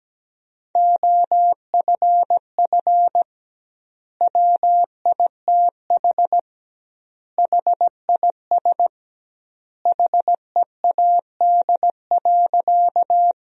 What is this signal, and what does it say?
Morse code, heard as a keyed tone, about 17 wpm, 700 Hz. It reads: OFF WITH HIS HEAD.
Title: 2x Word-Spacing - Sentences from Top 300 Words - 17wpm